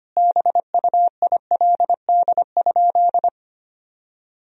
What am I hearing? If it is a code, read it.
Morse code, heard as a keyed tone, about 25 wpm, 700 Hz. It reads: BUILD?
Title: Sentences from Top 1000 Words - Review Part 3 - 25wpm